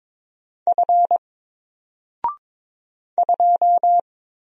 Morse code, heard as a keyed tone, about 22 wpm, 700 Hz. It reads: F 2
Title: Single Character (Letter:Number:Punctuation:Prosign) Alt 22wpm